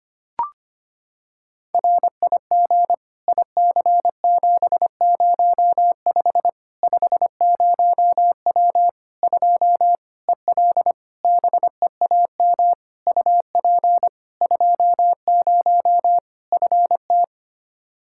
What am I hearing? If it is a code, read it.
Morse code, heard as a keyed tone, about 25 wpm, 700 Hz. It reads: RIG IC705 50W 2 EL BEAM UP 20 FT
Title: Practice QSO - Exchange 2 - Info - 25wpm